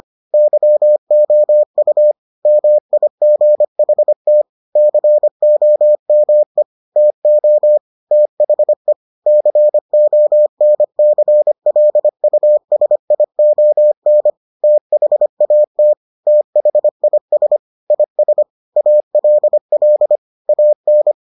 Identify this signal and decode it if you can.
Morse code, heard as a keyed tone, about 25 wpm, 600 Hz. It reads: YOU MIGHT COME TO THE CONCLUSION THAT THIS IS ALL AN